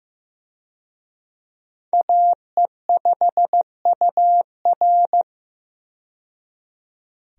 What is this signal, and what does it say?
Morse code, heard as a keyed tone, about 15 wpm, 700 Hz. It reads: AE5UR
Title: USA Call Signs - 5 Characters - 15wpm